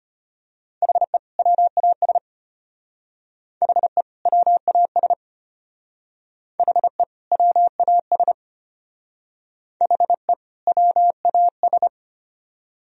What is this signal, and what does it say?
Morse code, heard as a keyed tone, about 32 wpm, 700 Hz. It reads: HE WAS HE WAS HE WAS HE WAS